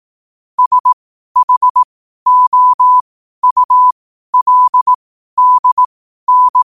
Morse code, heard as a keyed tone, about 18 wpm, spaced as for 12 wpm, 1000 Hz. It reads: SHOULDN
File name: qotdfiles / 12wpm